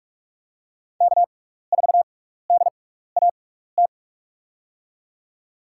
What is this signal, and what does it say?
Morse code, heard as a keyed tone, about 46 wpm, 700 Hz. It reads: K 4 D A T
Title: USA Call Signs - 5 Characters - 6x Character Spacing - 45wpm